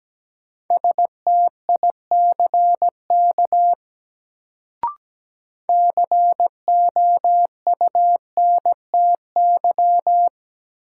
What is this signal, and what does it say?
Morse code, heard as a keyed tone, about 17 wpm, 700 Hz. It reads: STICK COUNTY
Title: Sentences from Top 1200 Words - New Word Drill - 17wpm